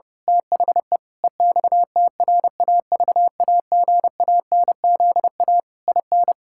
Morse code, heard as a keyed tone, about 30 wpm, 700 Hz. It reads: THE EXTRAVAGANZA IN
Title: Pangram Challenge v2 - 30wpm